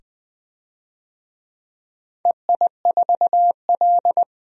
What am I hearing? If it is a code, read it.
Morse code, heard as a keyed tone, about 20 wpm, 700 Hz. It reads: EI4L
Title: DXCC-100 Call Signs - 4 Characters - 20wpm